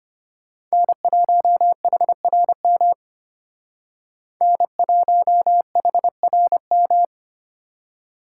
Morse code, heard as a keyed tone, about 32 wpm, 700 Hz. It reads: N1HRM N1HRM